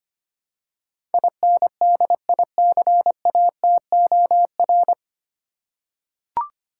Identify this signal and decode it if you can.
Morse code, heard as a keyed tone, about 25 wpm, 700 Hz. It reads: INDICATOR
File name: practice / 25wpm / 9 Letter Words - First 3 Letters Given - 25wpm